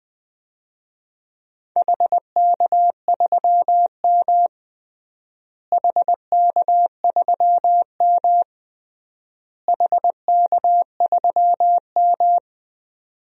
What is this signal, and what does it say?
Morse code, heard as a keyed tone, about 21 wpm, 700 Hz. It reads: HK3M HK3M HK3M
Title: DXCC-100 Call Signs - 4 Characters - Repeated 3x - 20wpm